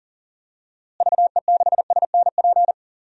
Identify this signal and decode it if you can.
Morse code, heard as a keyed tone, about 40 wpm, 700 Hz. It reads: VE6SNP
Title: DXCC-100 Call Signs - 6 Characters - Repeated 2x - 40wpm